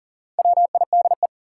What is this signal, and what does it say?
Morse code, heard as a keyed tone, about 38 wpm, 700 Hz. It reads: WIDE
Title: Sentences from Top 900 Words - New Word Drill - 40wpm